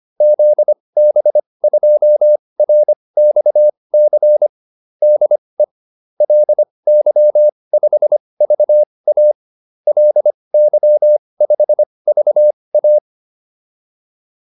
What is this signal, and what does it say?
Morse code, heard as a keyed tone, about 25 wpm, spaced as for 20 wpm, 600 Hz. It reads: ZB2RXC DE LY5VA LY5VA
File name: QSO406 20